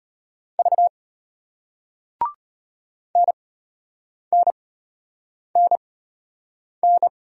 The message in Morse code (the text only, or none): V NNNN